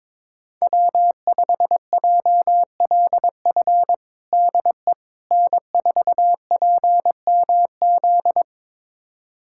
Morse code, heard as a keyed tone, about 22 wpm, 700 Hz. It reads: W5JLF DE N4PMZ
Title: USA Callsign DE Callsign - 5 Characters - 22wpm